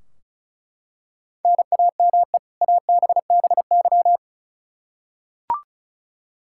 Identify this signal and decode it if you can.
Morse code, heard as a keyed tone, about 34 wpm, 700 Hz. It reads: NAME ABBY
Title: Practice QSO - Exchange 1 - Beginning - 35wpm